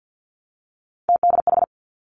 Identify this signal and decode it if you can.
Morse code, heard as a keyed tone, about 50 wpm, 700 Hz. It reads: TDH
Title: Three Challenging Letter-Number Combinations 50wpm